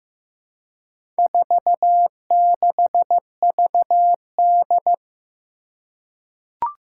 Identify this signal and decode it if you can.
Morse code, heard as a keyed tone, about 15 wpm, 700 Hz. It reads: 46VD
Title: Four Challenging Letter-Number Combinations 15wpm